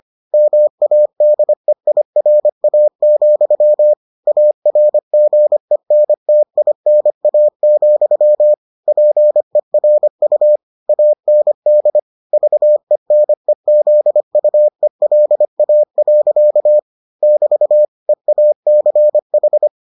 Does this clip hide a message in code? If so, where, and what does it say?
Yes: Morse code from 0.3 to 19.7 s, heard as a keyed tone, about 25 wpm, 600 Hz. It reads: MADEIRA, ARGENTINA, PERU AND VENEZUELA. <BT> EACH